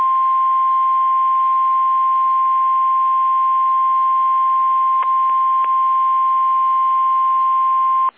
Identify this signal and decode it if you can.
Emergency Alert System (SAME) tones (the 1050 Hz attention tone). It is heard from 0.3 to 7.2 s.
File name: eas / KSDZKDJL / KSDZKDJL